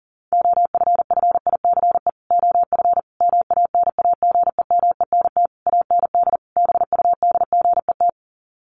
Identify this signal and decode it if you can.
Morse code, heard as a keyed tone, about 40 wpm, 700 Hz. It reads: OFFICE OF MANAGEMENT AND BUDGET
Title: Sets of 5 Words Encore2 40wpm